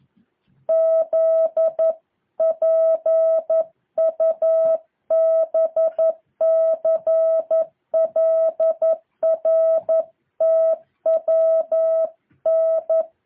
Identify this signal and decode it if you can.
Morse code, heard as a keyed tone, about 11 wpm, 650 Hz. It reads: ZPUBCLRTWN
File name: jh8cbh / 2023cw